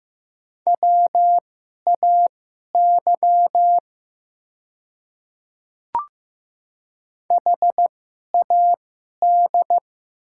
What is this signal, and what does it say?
Morse code, heard as a keyed tone, about 15 wpm, spaced as for 11 wpm, 700 Hz. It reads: WAY HAD
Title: 2x Character Spacing - Top 100 Words - 15wpm